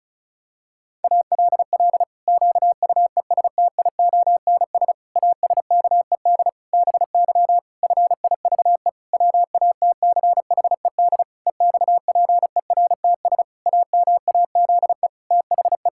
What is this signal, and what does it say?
Morse code, heard as a keyed tone, about 34 wpm, 700 Hz. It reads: ALL QUESTIONS ASKED BY FIVE WATCHED EXPERTS AMAZE THE
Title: Pangram Challenge v2 - 35wpm